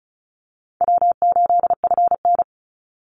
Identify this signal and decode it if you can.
Morse code, heard as a keyed tone, about 34 wpm, 700 Hz. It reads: W8FN
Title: Easy Call Signs Repeated 3x - 35wpm